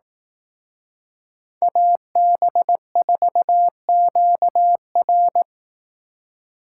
Morse code, heard as a keyed tone, about 18 wpm, 700 Hz. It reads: AB4QR
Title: Speed Racing - USA Call Signs 15wpm